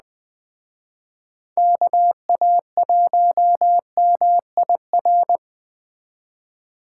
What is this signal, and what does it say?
Morse code, heard as a keyed tone, about 20 wpm, 700 Hz. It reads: KA1MIR
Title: Speed Racing - USA 2x3 Call Signs 17wpm